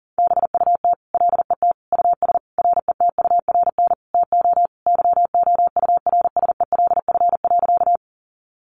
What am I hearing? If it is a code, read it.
Morse code, heard as a keyed tone, about 40 wpm, 700 Hz. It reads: BUT LET US RETURN TO YOURSELF.
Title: Sentences from Top 800 Words 40wpm